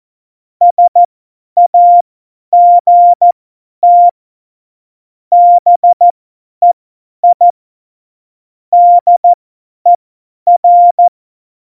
Morse code, heard as a keyed tone, about 14 wpm, spaced as for 10 wpm, 700 Hz. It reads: SAGT BEI DER
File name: Morsekurs / Witze / Witz 20. Oktober 2020, 50 F70